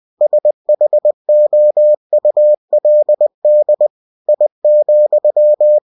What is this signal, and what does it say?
Morse code, heard as a keyed tone, about 20 wpm, 600 Hz. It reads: SHOULD I,
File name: cw-books / winnie-the-pooh / winnie-20wpm-page-0017